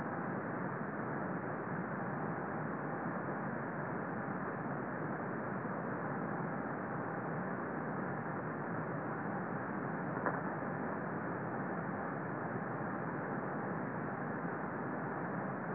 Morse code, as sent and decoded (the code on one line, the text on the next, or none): none